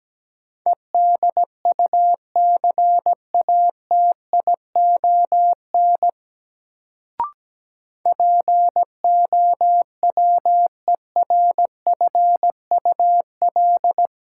Morse code, heard as a keyed tone, about 17 wpm, 700 Hz. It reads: EDUCATION POWERFUL
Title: Sentences from Top 1000 Words - Review Part 6 - New Word Drill - 17wpm